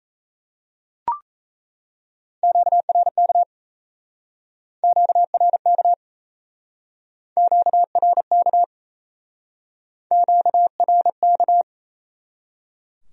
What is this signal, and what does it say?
Morse code, heard as a keyed tone, about 39 wpm, 700 Hz. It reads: QRK QRK QRK QRK